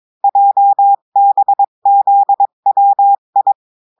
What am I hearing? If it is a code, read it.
Morse code, heard as a keyed tone, about 22 wpm, 800 Hz. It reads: JBZWI
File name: cours-du-mercredi / audios / cours-853-du-2-mai-2024-etxte chiffre et texte clair-a-20-wpm